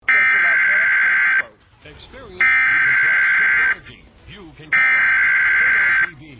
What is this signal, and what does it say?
Emergency Alert System (SAME) tones (header bursts), heard as harsh buzzing bursts, with speech. It is heard from 0.1 to 6.0 s.
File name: hacked-eas-alert